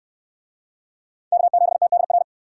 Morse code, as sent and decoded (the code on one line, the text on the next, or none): -.- --... - -.. .-.
K7TDR